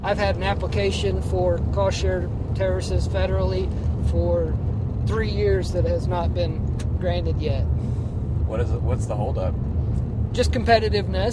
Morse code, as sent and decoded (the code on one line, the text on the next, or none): none